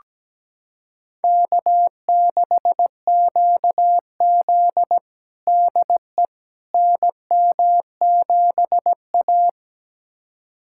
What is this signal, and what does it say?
Morse code, heard as a keyed tone, about 17 wpm, 700 Hz. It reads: K6QZ DE NM7A
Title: USA Callsign DE Callsign - 4 Characters - 17wpm